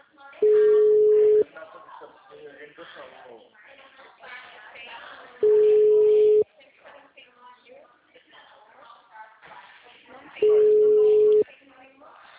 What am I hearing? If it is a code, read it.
Ringback tone (Europe, 425 Hz), with speech.
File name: out- -2018- - - .19355